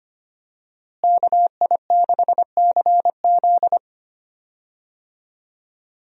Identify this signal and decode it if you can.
Morse code, heard as a keyed tone, about 25 wpm, 700 Hz. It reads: KI6CZ
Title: USA Call Signs - 5 Characters - Repeated 2x - 25wpm